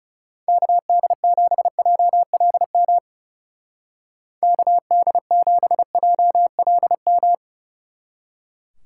Morse code, heard as a keyed tone, about 38 wpm, 700 Hz. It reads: KD7JLM KD7JLM